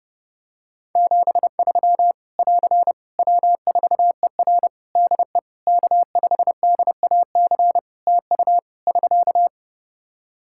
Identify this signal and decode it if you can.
Morse code, heard as a keyed tone, about 30 wpm, 700 Hz. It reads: 73 <AR> W4ER DE K5DAC TU <SK>
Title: K1USN SST QSO Practice 30wpm